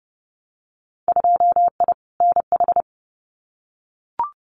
Morse code, heard as a keyed tone, about 30 wpm, 700 Hz. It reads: 2I NH
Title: Winter Field Day Exchanges Encore 30wpm